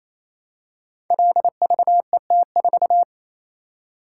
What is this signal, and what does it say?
Morse code, heard as a keyed tone, about 28 wpm, 700 Hz. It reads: LVET4
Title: Code Groups - 5 Letter:Numbers - 28wpm